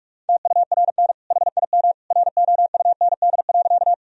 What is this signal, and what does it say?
Morse code, heard as a keyed tone, about 41 wpm, 700 Hz. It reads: TURN HIM ROUND.
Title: Sentences from Top 1000 Words - Review Part 1 - 45wpm